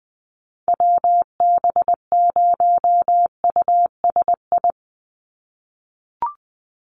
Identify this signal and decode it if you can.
Morse code, heard as a keyed tone, about 20 wpm, 700 Hz. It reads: WB0USI
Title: NA SOTA Chaser Callsigns 20wpm